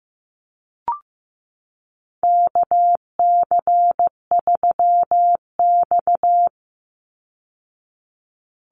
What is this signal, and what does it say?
Morse code, heard as a keyed tone, about 15 wpm, 700 Hz. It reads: KC3X
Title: Call Signs Repeated 3x - 15wpm